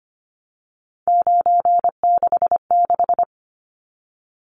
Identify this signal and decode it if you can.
Morse code, heard as a keyed tone, about 25 wpm, 700 Hz. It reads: 966
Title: Numbers - 3-Digit 25wpm